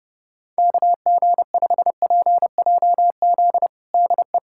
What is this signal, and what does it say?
Morse code, heard as a keyed tone, about 32 wpm, 700 Hz. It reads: KG5PJZ DE
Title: USA Callsign DE Callsign - 6 Characters - 30wpm